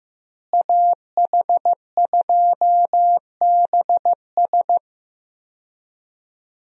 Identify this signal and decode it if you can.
Morse code, heard as a keyed tone, about 15 wpm, 700 Hz. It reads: AH2BS
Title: USA Call Signs - 5 Characters - Repeated 3x - 15wpm